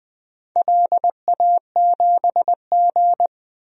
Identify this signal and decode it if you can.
Morse code, heard as a keyed tone, about 21 wpm, 700 Hz. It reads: LA7G
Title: DXCC-100 Call Signs - 4 Characters - Repeated 3x - 20wpm